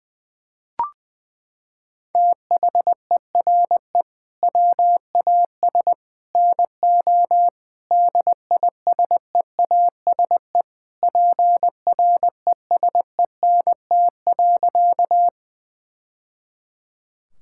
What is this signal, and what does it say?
Morse code, heard as a keyed tone, about 20 wpm, 700 Hz. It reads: THERE WAS NO DISEASE PRESENT.
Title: Sentences from Top 1000 Words - Review Part 6 - 20wpm